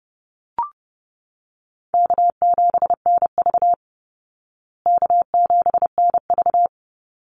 Morse code, heard as a keyed tone, about 30 wpm, 700 Hz. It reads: K7NV K7NV
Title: Easy Call Signs Repeated 3x - 30wpm